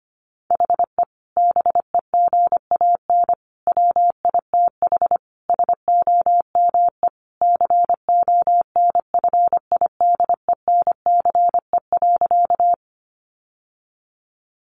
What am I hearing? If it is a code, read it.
Morse code, heard as a keyed tone, about 25 wpm, 700 Hz. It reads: HE BEGAN WITH SOME CONFIDENCE.
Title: Sentences from Top 1500 Words 25wpm